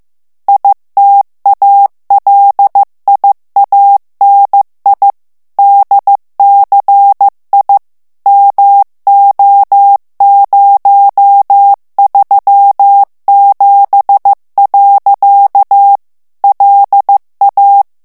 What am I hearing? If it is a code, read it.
Morse code, heard as a keyed tone, about 15 wpm, 800 Hz. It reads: ITALIANI DCI MO037. LA
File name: ITALIEN texte clair 1 vitesse 14 mots minute